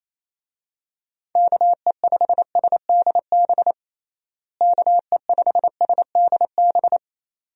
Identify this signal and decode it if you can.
Morse code, heard as a keyed tone, about 27 wpm, 700 Hz. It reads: KE5SDB KE5SDB
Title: USA Call Signs - 6 Characters - Repeated 2x - 28wpm